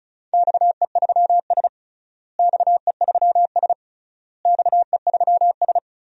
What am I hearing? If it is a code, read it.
Morse code, heard as a keyed tone, about 34 wpm, 700 Hz. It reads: XE3S XE3S XE3S
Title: DXCC-100 Call Signs - 4 Characters - Repeated 3x - 35wpm